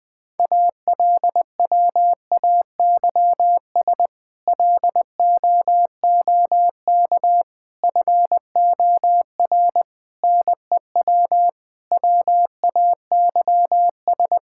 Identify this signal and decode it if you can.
Morse code, heard as a keyed tone, about 20 wpm, 700 Hz. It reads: ALWAYS LOOK FOR NEW WAYS
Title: Sentences with 6-Word Phrases - Spoken Beginning - 20wpm